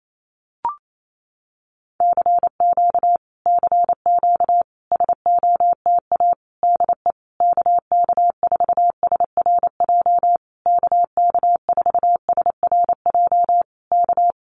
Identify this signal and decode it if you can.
Morse code, heard as a keyed tone, about 28 wpm, 700 Hz. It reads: CQ CQ SOTA DE KK4SRJ KK4SRJ K
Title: Common QSO Phrases Encore 28wpm